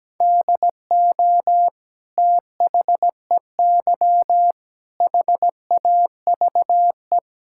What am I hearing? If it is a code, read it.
Morse code, heard as a keyed tone, about 17 wpm, 700 Hz. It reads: DO THEY HAVE
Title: Sentences from Top 1000 Words - Review Part 6 - 17wpm